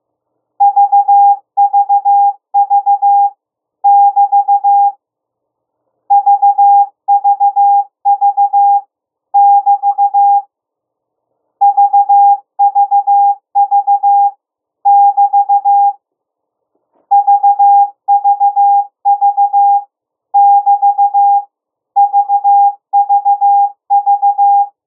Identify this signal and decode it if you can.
Morse code, heard as a keyed tone, about 15 wpm, 810 Hz. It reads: VVV <BT> VVV <BT> VVV <BT> VVV <BT> VVV